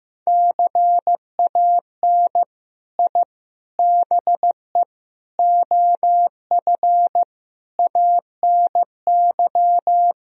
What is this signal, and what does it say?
Morse code, heard as a keyed tone, about 15 wpm, 700 Hz. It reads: CAN I BE OF ANY
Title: Sentences from Top 1000 Words - Review Part 4 - 15wpm